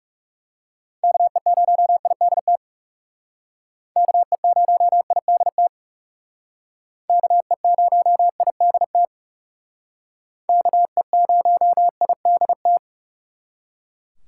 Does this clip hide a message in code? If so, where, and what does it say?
Yes: Morse code from 1.0 to 12.8 s, heard as a keyed tone, about 40 wpm, 700 Hz. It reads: KE0IDT KE0IDT KE0IDT KE0IDT